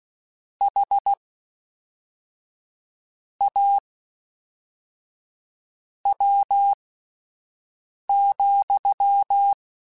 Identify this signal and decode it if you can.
Morse code, heard as a keyed tone, about 16 wpm, spaced as for 4 wpm, 800 Hz. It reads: HAW,
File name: Misc / Misc-mp3s / WordPairs sp @5